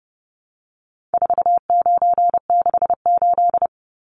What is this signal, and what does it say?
Morse code, heard as a keyed tone, about 29 wpm, 700 Hz. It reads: 4968
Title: Numbers - 4-Digit 30wpm